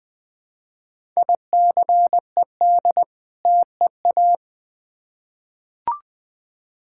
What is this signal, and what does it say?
Morse code, heard as a keyed tone, about 21 wpm, 700 Hz. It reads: ICED TEA
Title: Difficult Words with Hints - 20wpm